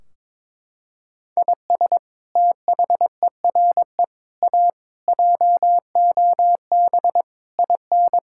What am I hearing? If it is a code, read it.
Morse code, heard as a keyed tone, about 22 wpm, 700 Hz. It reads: IS THERE A JOB IN
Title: Sentences from Top 1600 Words 22wpm